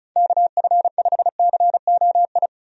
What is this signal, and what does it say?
Morse code, heard as a keyed tone, about 36 wpm, 700 Hz. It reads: KF5COI
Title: USA Call Signs - 6 Characters - 35wpm